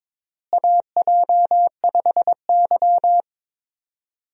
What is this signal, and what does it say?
Morse code, heard as a keyed tone, about 22 wpm, 700 Hz. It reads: AJ5Y
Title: USA Call Signs - 4 Characters - Repeated 3x - 22wpm